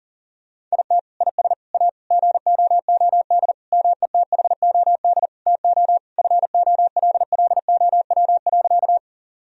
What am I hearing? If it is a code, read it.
Morse code, heard as a keyed tone, about 40 wpm, 700 Hz. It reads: IT IS A GOOD METHOD TO FOLLOW.
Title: Sentences from Top 1000 Words - Review Part 3 - 40wpm